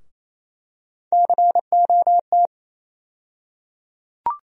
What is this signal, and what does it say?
Morse code, heard as a keyed tone, about 28 wpm, 700 Hz. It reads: COT
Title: USA Call Sign Postfixes - 3 Letters 28wpm